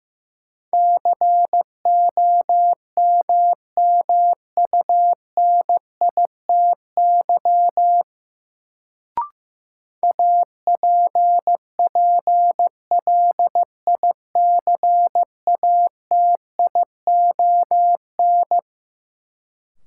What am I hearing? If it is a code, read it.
Morse code, heard as a keyed tone, about 15 wpm, 700 Hz. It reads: COMMUNITY APPLICATION
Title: Sentences from Top 1200 Words - New Word Drill Alt - 15wpm